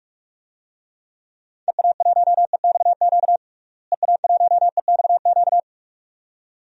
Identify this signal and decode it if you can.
Morse code, heard as a keyed tone, about 43 wpm, 700 Hz. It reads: EA1EXQ EA1EXQ
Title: DXCC-100 Call Signs - 6 Characters - Repeated 2x - 45wpm